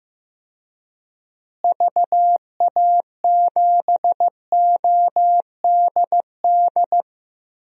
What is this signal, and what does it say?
Morse code, heard as a keyed tone, about 15 wpm, 700 Hz. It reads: VA7ODD
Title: DXCC-100 Call Signs - 6 Characters - Repeated 3x - 15wpm